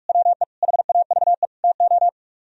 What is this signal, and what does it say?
Morse code, heard as a keyed tone, about 44 wpm, 700 Hz. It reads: WE HAVE TO